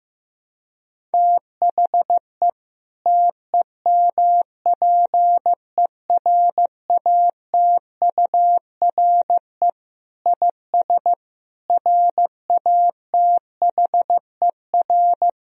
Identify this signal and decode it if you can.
Morse code, heard as a keyed tone, about 15 wpm, 700 Hz. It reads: THE TEMPERATURE IS RATHER
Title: Sentences from Top 1000 Words - Review Part 4 - 15wpm